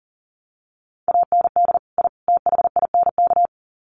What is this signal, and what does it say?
Morse code, heard as a keyed tone, about 39 wpm, 700 Hz. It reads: AND I THINK
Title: Sets of 3 Words 40wpm